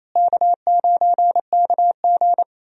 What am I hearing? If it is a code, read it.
Morse code, heard as a keyed tone, about 28 wpm, 700 Hz. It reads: K9KG
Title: USA Call Signs - 4 Characters - Repeated 3x - 28wpm